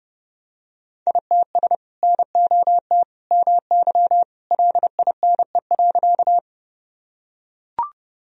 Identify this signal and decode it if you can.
Morse code, heard as a keyed tone, about 30 wpm, 700 Hz. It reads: ITS NOT MY LINE.
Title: Sentences from Top 1000 Words - Review Part 1 - 30wpm